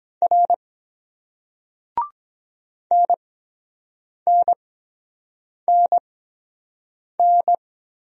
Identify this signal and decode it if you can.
Morse code, heard as a keyed tone, about 23 wpm, 700 Hz. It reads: R N N N N